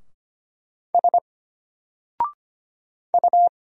Morse code, heard as a keyed tone, about 25 wpm, 700 Hz. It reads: S U
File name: Single-Letter-Number-25wpm